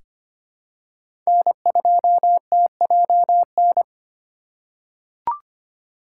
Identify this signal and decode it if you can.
Morse code, heard as a keyed tone, about 25 wpm, 700 Hz. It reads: N2TJN
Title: USA Call Signs - 5 Characters - 2x Character Spacing - 25wpm